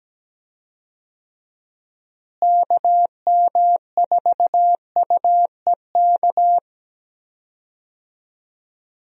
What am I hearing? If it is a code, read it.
Morse code, heard as a keyed tone, about 17 wpm, 700 Hz. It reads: KM4UEK
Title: USA Call Signs - 6 Characters - Repeated 3x - 17wpm